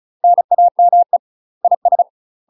Morse code, heard as a keyed tone, about 36 wpm, 700 Hz. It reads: NAME IS
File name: UK MASTER QSOs TRAINING / 35WPM / UK MASTER QSOs 0044